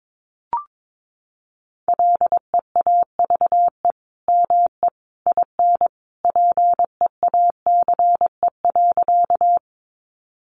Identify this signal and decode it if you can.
Morse code, heard as a keyed tone, about 22 wpm, 700 Hz. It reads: LEAVE ME IN PEACE.
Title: Sentences from Top 1500 Words 22wpm